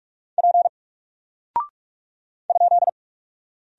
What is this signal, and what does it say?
Morse code, heard as a keyed tone, about 45 wpm, 700 Hz. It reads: P ?
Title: Single Character (Letter:Number:Punctuation:Prosign) 45wpm